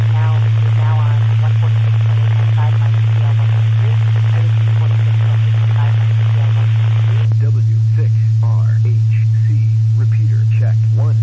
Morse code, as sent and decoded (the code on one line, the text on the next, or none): none